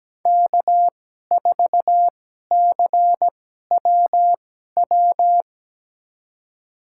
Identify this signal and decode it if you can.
Morse code, heard as a keyed tone, about 17 wpm, spaced as for 12 wpm, 700 Hz. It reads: K4CWW
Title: USA Call Signs - 5 Characters - 2x Character Spacing - 17wpm